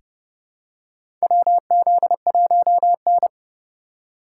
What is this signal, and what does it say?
Morse code, heard as a keyed tone, about 30 wpm, 700 Hz. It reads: WZ1N
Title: USA Call Signs - 3x Character Spacing - 30wpm